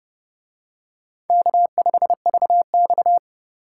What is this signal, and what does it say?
Morse code, heard as a keyed tone, about 29 wpm, 700 Hz. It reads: K5VX